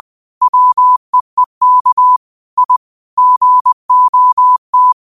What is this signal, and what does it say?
Morse code, heard as a keyed tone, about 20 wpm, 1000 Hz. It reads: WEEK I GOT